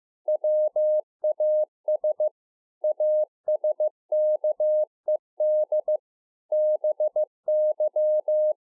Morse code, heard as a keyed tone, about 15 wpm, 610 Hz. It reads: WAS ASKED BY